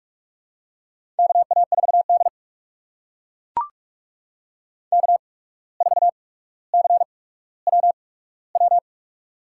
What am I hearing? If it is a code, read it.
Morse code, heard as a keyed tone, about 44 wpm, spaced as for 18 wpm, 700 Hz. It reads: KA4D K4CWW
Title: USA Call Signs - 8x Character Spacing - 45wpm